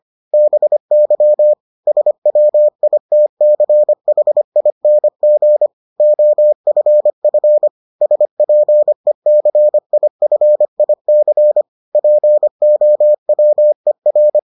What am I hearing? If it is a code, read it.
Morse code, heard as a keyed tone, about 25 wpm, 600 Hz. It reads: BY SWITCHING OFF SPECIFIC POWER